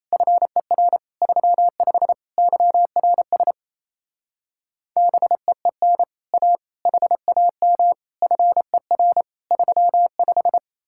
Morse code, heard as a keyed tone, about 37 wpm, 700 Hz. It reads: FER 35 YRS BEEN A HAM FER 35